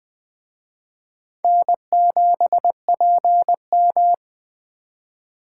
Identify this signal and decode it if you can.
Morse code, heard as a keyed tone, about 23 wpm, 700 Hz. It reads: N7PM
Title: Speed Racing - USA 1x2 Call Signs 20wpm